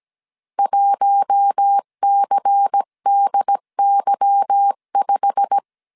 Morse code, heard as a keyed tone, about 17 wpm, 800 Hz. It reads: 1CDY5
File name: CW / DET / 1 9 1500C S B 16wpm 2856